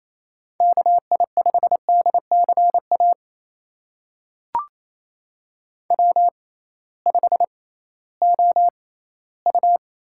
Morse code, heard as a keyed tone, about 28 wpm, spaced as for 16 wpm, 700 Hz. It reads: KI5DCA W5OU
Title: USA Call Signs - 6x Character Spacing - 28wpm